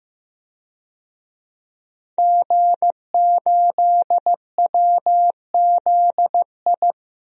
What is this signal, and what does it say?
Morse code, heard as a keyed tone, about 15 wpm, 700 Hz. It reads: G8WZI
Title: DXCC-100 Call Signs - 5 Characters - 15wpm